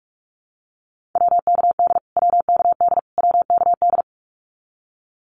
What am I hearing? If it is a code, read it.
Morse code, heard as a keyed tone, about 44 wpm, 700 Hz. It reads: WKD WKD WKD
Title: Abbreviations Repeated 3x - 45wpm